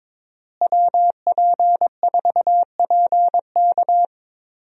Morse code, heard as a keyed tone, about 22 wpm, 700 Hz. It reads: WP4PK
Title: USA Call Signs - 5 Characters - Repeated 2x - 22wpm